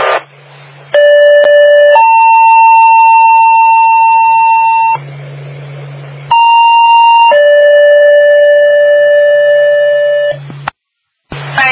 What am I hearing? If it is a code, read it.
Two-tone page, heard as a one-second tone, then a three-second tone, about 580 then 900 Hz, about 930 then 580 Hz. It is heard from 0.9 to 10.3 s.